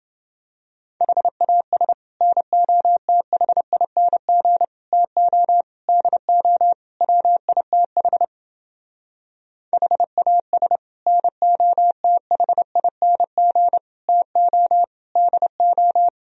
Morse code, heard as a keyed tone, about 28 wpm, 700 Hz. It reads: HAS NOTHING TO DO WITH HAS NOTHING TO DO